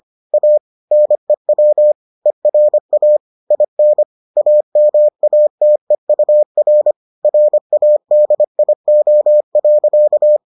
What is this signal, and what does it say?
Morse code, heard as a keyed tone, about 25 wpm, 600 Hz. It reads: A NEW ERA IN AMATEUR RADIO.